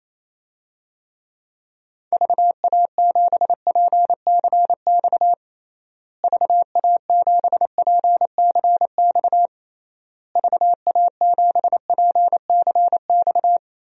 Morse code, heard as a keyed tone, about 28 wpm, 700 Hz. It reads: VA7PCX VA7PCX VA7PCX
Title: DXCC-100 Call Signs - 6 Characters - Repeated 3x - 28wpm